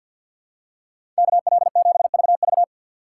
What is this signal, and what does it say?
Morse code, heard as a keyed tone, about 49 wpm, 700 Hz. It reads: KL7VV
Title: USA Call Signs - 5 Characters - 6x Character Spacing - 50wpm